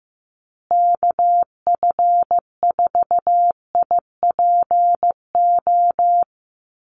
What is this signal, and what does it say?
Morse code, heard as a keyed tone, about 15 wpm, 700 Hz. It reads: KF4IPO
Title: USA Call Signs Repeated 2x 15wpm